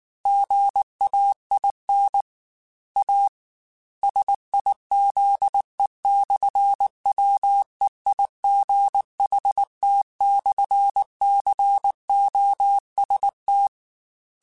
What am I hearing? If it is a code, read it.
Morse code, heard as a keyed tone, about 19 wpm, 800 Hz. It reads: GAIN A SIZE/WEIGHT/COST
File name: Technical Subjects / Technical Subjects-mp3s / Sw Pwr Sup @18